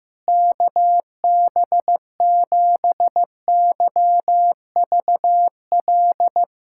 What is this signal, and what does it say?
Morse code, heard as a keyed tone, about 15 wpm, 700 Hz. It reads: KB7YVL
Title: USA Callsign DE Callsign - 6 Characters - 15wpm